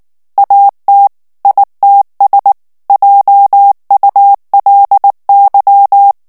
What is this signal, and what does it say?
Morse code, heard as a keyed tone, about 19 wpm, 800 Hz. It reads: AT ITS JULY
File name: ANGLAIS texte clair 3 vitesse 18 mots minute